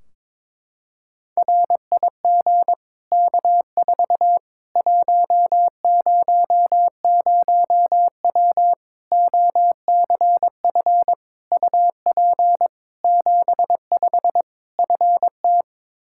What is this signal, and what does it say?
Morse code, heard as a keyed tone, about 22 wpm, 700 Hz. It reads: RIG K4 100W OCF UP 75 FT
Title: Practice QSO - Exchange 2 - Info - 22wpm